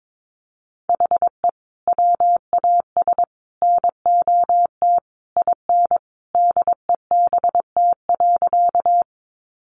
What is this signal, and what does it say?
Morse code, heard as a keyed tone, about 22 wpm, 700 Hz. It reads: HE WAS NOT IN DEBT.
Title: Sentences from Top 1300 Words 22wpm